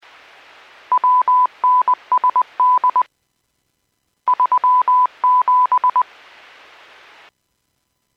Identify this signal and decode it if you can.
Morse code, heard as a keyed tone, about 20 wpm, 1000 Hz. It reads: WNSD 37
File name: Businesses 451.25